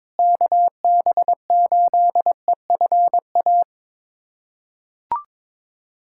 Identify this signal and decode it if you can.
Morse code, heard as a keyed tone, about 22 wpm, 700 Hz. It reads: KB8EFA
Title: USA Callsign DE Callsign - 6 Characters - 22wpm